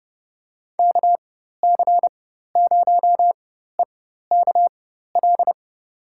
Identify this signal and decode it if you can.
Morse code, heard as a keyed tone, about 30 wpm, spaced as for 14 wpm, 700 Hz. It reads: KC0EKL
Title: USA Call Signs - 6 Characters - 4x Character Spacing - 30wpm